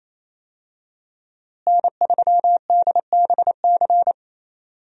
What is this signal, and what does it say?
Morse code, heard as a keyed tone, about 28 wpm, 700 Hz. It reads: N3DBC
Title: USA Call Signs - 5 Characters - Repeated 2x - 28wpm